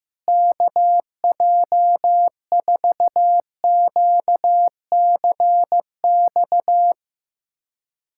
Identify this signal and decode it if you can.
Morse code, heard as a keyed tone, about 15 wpm, 700 Hz. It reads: KJ4QCX
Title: USA Callsign DE Callsign - 6 Characters - 15wpm